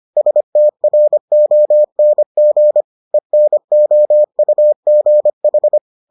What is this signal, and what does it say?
Morse code, heard as a keyed tone, about 25 wpm, 600 Hz. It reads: STRONG ENOUGH